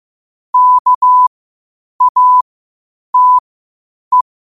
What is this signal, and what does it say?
Morse code, heard as a keyed tone, about 15 wpm, spaced as for 8 wpm, 1000 Hz. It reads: KATE
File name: qotdfiles / 8wpm